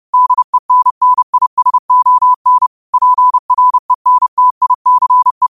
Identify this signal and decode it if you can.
Morse code, heard as a keyed tone, about 30 wpm, 1000 Hz. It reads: DENNISON PRENTICE